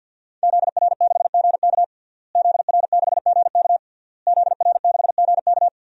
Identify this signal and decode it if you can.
Morse code, heard as a keyed tone, about 53 wpm, 700 Hz. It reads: ZR6GK ZR6GK ZR6GK
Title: DXCC-100 Call Signs - 5 Characters - Repeated 3x - 50wpm